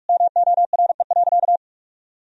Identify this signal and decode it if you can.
Morse code, heard as a keyed tone, about 46 wpm, 700 Hz. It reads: MORE.
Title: Sentences from Top 1000 Words - Review Part 1 - 45wpm